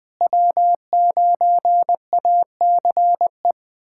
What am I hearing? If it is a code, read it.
Morse code, heard as a keyed tone, about 20 wpm, 700 Hz. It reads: W9ACE
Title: USA Callsign DE Callsign - 5 Characters - 20wpm